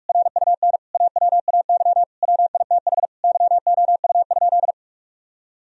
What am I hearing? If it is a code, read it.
Morse code, heard as a keyed tone, about 41 wpm, 700 Hz. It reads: RUN AWAY WITH YOU?
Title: Sentences from Top 1000 Words - Review Part 3 - 45wpm